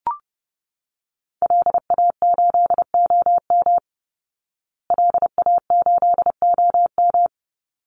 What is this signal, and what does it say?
Morse code, heard as a keyed tone, about 30 wpm, 700 Hz. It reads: LA8OM LA8OM
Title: Call Signs Repeated 3x - 30wpm